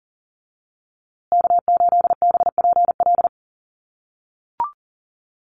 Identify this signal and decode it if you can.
Morse code, heard as a keyed tone, about 40 wpm, 700 Hz. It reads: K8BPL
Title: USA Call Signs Repeated 2x 40wpm